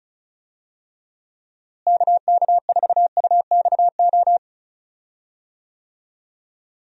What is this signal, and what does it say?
Morse code, heard as a keyed tone, about 34 wpm, 700 Hz. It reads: KK4UXO
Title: USA Call Signs - 6 Characters - Repeated 3x - 35wpm